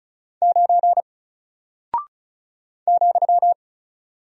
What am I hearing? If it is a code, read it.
Morse code, heard as a keyed tone, about 36 wpm, 700 Hz. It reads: 9 ,
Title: Single Character (Letter:Number:Punctuation:Prosign) Alt 35wpm